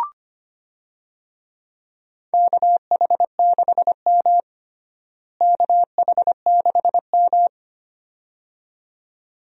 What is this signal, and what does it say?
Morse code, heard as a keyed tone, about 25 wpm, 700 Hz. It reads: KH6M KH6M
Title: DXCC-100 Call Signs - 4 Characters - Repeated 2x - 25wpm